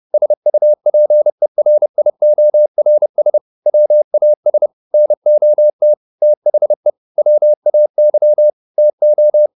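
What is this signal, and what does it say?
Morse code, heard as a keyed tone, about 30 wpm, 600 Hz. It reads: SUPERIORS WAS NOT THE WAY TO
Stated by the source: Morse code at 30 wpm, the novel Never by Ken Follett in English, from CW med Gnister.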